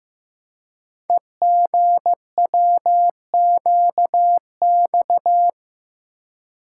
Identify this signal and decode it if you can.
Morse code, heard as a keyed tone, about 15 wpm, 700 Hz. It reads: EGWQX
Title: Code Groups - 5 Letters - 15wpm